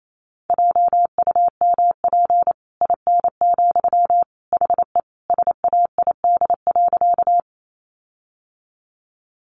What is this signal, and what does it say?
Morse code, heard as a keyed tone, about 28 wpm, 700 Hz. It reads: JUMP IN, HE SAID.
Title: Sentences from Top 1300 Words 28wpm